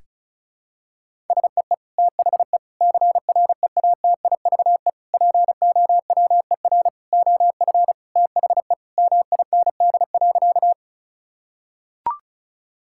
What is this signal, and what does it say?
Morse code, heard as a keyed tone, about 36 wpm, 700 Hz. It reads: SEE THE CREATIVE POWER OF THE MIND.
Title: Sentences from Top 1000 Words - Review Part 5 - 35wpm